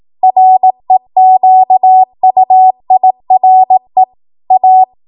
Morse code, heard as a keyed tone, about 18 wpm, 750 Hz. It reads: REQUIRE A
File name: Morse / 18WPM